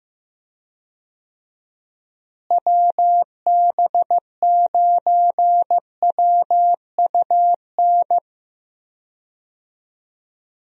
Morse code, heard as a keyed tone, about 15 wpm, 700 Hz. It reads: WB9WUN
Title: USA Call Signs - 6 Characters - Repeated 2x - 15wpm